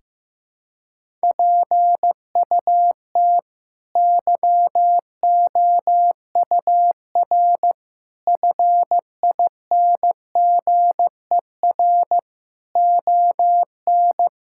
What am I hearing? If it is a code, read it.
Morse code, heard as a keyed tone, about 16 wpm, 700 Hz. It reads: PUT YOUR FINGER ON
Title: Sentences from Top 1600 Words 15wpm